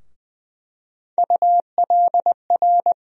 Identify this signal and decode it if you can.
Morse code, heard as a keyed tone, about 20 wpm, 700 Hz. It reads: ULR
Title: USA Call Sign Postfixes - 3 Letters 20wpm